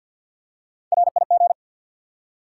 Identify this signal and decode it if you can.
Morse code, heard as a keyed tone, about 52 wpm, 700 Hz. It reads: RIG
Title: Top 90 QSO Elements 50wpm